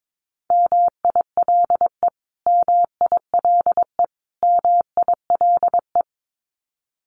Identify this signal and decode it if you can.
Morse code, heard as a keyed tone, about 22 wpm, 700 Hz. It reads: MILE MILE MILE
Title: Top 500 Words Repeated 22wpm